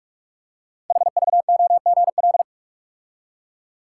Morse code, heard as a keyed tone, about 44 wpm, 700 Hz. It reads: HVOGL
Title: Code Groups - 5 Letter:Numbers - 45wpm